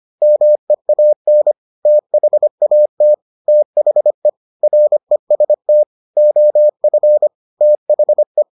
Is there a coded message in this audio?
Yes: Morse code from 0.2 to 8.4 s, heard as a keyed tone, about 25 wpm, 600 Hz. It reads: MEAN THAT THE REST OF THE